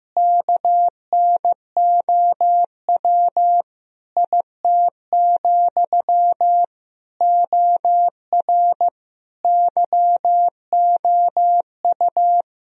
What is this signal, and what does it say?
Morse code, heard as a keyed tone, about 15 wpm, 700 Hz. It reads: KNOW IT, OR YOU
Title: Sentences from Top 1000 Words - Review Part 4 - 15wpm